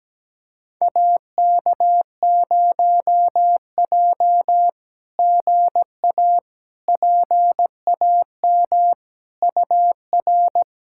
Morse code, heard as a keyed tone, about 17 wpm, 700 Hz. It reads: AK0J GA PAM UR
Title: SOTA QSO Practice 17wpm